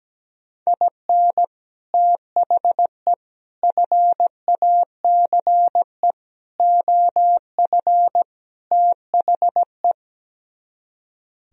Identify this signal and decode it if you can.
Morse code, heard as a keyed tone, about 17 wpm, 700 Hz. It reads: IN THE FACE OF THE
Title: Speed-Racing - Sets of 5 Words 17wpm